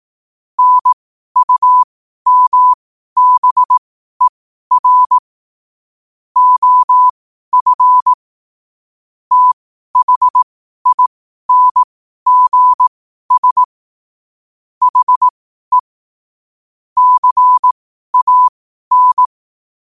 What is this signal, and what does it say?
Morse code, heard as a keyed tone, about 18 wpm, spaced as for 12 wpm, 1000 Hz. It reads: NUMBER OF THINGS HE CAN